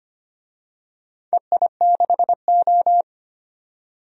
Morse code, heard as a keyed tone, about 25 wpm, 700 Hz. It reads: EI6O
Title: DXCC-100 Call Signs - 4 Characters - 25wpm